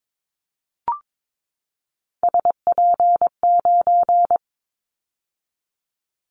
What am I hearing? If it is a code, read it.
Morse code, heard as a keyed tone, about 22 wpm, 700 Hz. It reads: SP9
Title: Callsign Prefixes for Top 25 DXCC - 22wpm